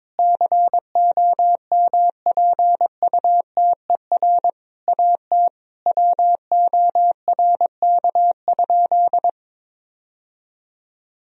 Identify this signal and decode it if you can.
Morse code, heard as a keyed tone, about 22 wpm, 700 Hz. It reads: COMPUTER AT WORK?
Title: Sentences from Top 1000 Words - Review Part 3 - 22wpm